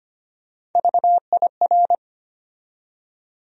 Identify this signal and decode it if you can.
Morse code, heard as a keyed tone, about 25 wpm, 700 Hz. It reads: VIR
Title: USA Call Sign Postfixes - 3 Letters 25wpm